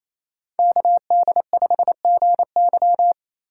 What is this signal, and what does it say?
Morse code, heard as a keyed tone, about 37 wpm, 700 Hz. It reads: KD5GY